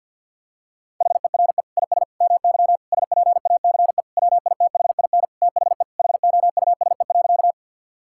Morse code, heard as a keyed tone, about 51 wpm, 700 Hz. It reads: HERE IS MY SPACE WITHIN THE HOUSE.
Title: Sentences from Top 1000 Words - Review Part 5 - 50wpm